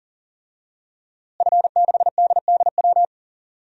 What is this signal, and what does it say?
Morse code, heard as a keyed tone, about 40 wpm, 700 Hz. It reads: F6DDW
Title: DXCC-100 Call Signs - 5 Characters - Repeated 2x - 40wpm